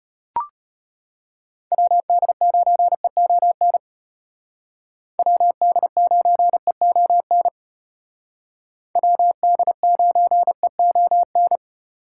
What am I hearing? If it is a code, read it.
Morse code, heard as a keyed tone, about 32 wpm, 700 Hz. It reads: WD9EON WD9EON WD9EON